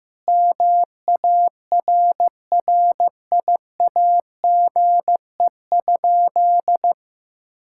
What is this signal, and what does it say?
Morse code, heard as a keyed tone, about 15 wpm, 700 Hz. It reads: MARRIAGE?
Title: Sentences from Top 1000 Words - Review Part 5 - 15wpm